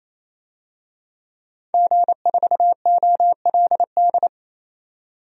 Morse code, heard as a keyed tone, about 28 wpm, 700 Hz. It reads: G4OLD
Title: DXCC-100 Call Signs - 5 Characters - 28wpm